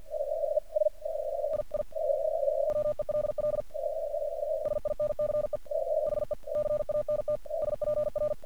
Morse code, heard as a keyed tone, about 48 wpm, 1200 Hz, with noise. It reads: I I WELL SINCE HE CANT SPR
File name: archived / CRSnet